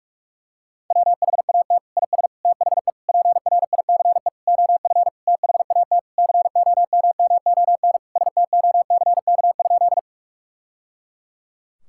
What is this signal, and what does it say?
Morse code, heard as a keyed tone, about 46 wpm, 700 Hz. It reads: WHAT IS THE PRICE OF THAT COMMON STOCK?
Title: Sentences from Top 1000 Words - Review Part 5 - 45wpm